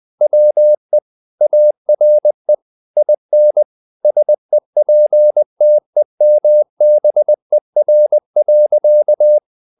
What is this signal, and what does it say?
Morse code, heard as a keyed tone, about 20 wpm, 600 Hz. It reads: WE ARE IN SEPTEMBER.
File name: the old man and the sea / the old man and the sea part 1 20wpm